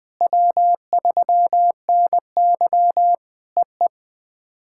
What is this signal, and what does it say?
Morse code, heard as a keyed tone, about 20 wpm, 700 Hz. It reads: W3NY EE
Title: POTA QSO Practice - 20wpm